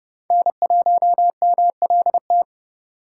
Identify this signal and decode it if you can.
Morse code, heard as a keyed tone, about 35 wpm, 700 Hz. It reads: N1MLT